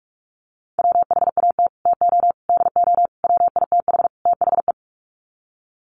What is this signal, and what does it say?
Morse code, heard as a keyed tone, about 44 wpm, 690 Hz. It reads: WHAT TO DO WITH THE
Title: Sets of 5 Words Encore2 45wpm